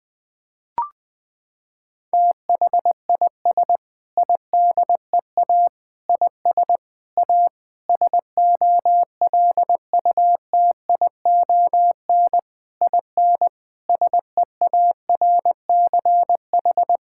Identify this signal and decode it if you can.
Morse code, heard as a keyed tone, about 20 wpm, 700 Hz. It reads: THIS IDEA IS A SOLUTION IN SEARCH
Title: Sentences with 8-Word Phrases - 20wpm